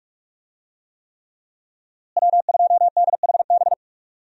Morse code, heard as a keyed tone, about 43 wpm, 700 Hz. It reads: W2DHB
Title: USA Call Signs - 5 Characters - Repeated 3x - 45wpm